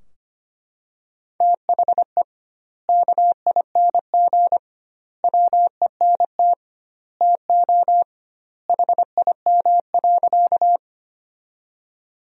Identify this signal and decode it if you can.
Morse code, heard as a keyed tone, about 25 wpm, 700 Hz. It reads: THE KING WENT TO HIM.
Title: 2x Word-Spacing - Sentences from Top 400 Words - 25wpm